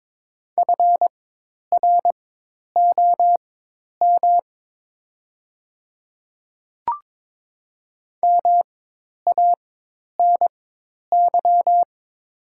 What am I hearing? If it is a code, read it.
Morse code, heard as a keyed tone, about 22 wpm, spaced as for 10 wpm, 700 Hz. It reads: FROM MANY
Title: 4x Character Spacing - Top 100 Words - 22wpm